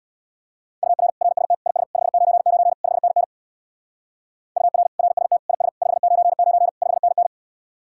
Morse code, heard as a keyed tone, about 60 wpm, 700 Hz. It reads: UR RST IS 599 5NN UR RST IS 599 5NN
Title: Speed-Racing Common QSO Phrases - 50wpm